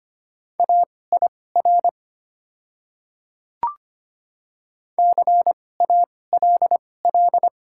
Morse code, heard as a keyed tone, about 25 wpm, spaced as for 18 wpm, 700 Hz. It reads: AIR CALL
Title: 2x Character Spacing - Top 300 Words - 25wpm